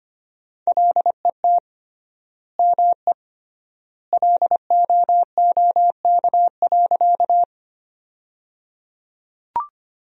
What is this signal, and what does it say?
Morse code, heard as a keyed tone, about 25 wpm, 700 Hz. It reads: LET ME LOOK.
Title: 3x Word-Spacing - Sentences from Top 300 Words - 25wpm